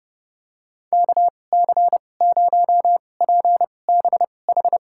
Morse code, heard as a keyed tone, about 30 wpm, spaced as for 22 wpm, 700 Hz. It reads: KC0PBH
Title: USA Call Signs - 6 Characters - 2x Character Spacing - 30wpm